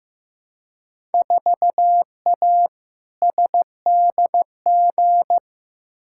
Morse code, heard as a keyed tone, about 15 wpm, 700 Hz. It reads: 4A SDG
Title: Field Day Exchanges Encore - 15wpm